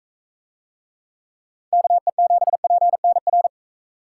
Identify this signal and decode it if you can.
Morse code, heard as a keyed tone, about 43 wpm, 700 Hz. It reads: KE7PNR